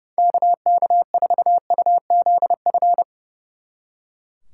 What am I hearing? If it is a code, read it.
Morse code, heard as a keyed tone, about 37 wpm, 700 Hz. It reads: KK4UZF
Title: Speed Racing - USA Call Signs 30wpm